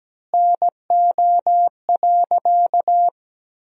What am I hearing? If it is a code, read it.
Morse code, heard as a keyed tone, about 17 wpm, 700 Hz. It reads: NO.
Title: 4x Word-Spacing - Sentences from Top 100 Words - 17wpm